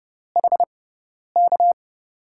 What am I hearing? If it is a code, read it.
Morse code, heard as a keyed tone, about 30 wpm, spaced as for 10 wpm, 700 Hz. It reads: HK
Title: USA Call Signs - 6 Characters - 6x Character Spacing - 30wpm